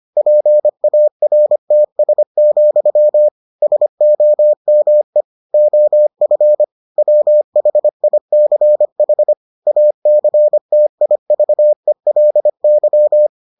Morse code, heard as a keyed tone, about 25 wpm, 600 Hz. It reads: PARTS, SOME OF WHICH ACTIVELY